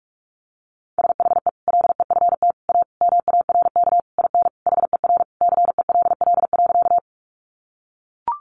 Morse code, heard as a keyed tone, about 44 wpm, 700 Hz. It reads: SHE LEFT A MARK IN HER CELL.
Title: Sentences from Top 1100 Words 45wpm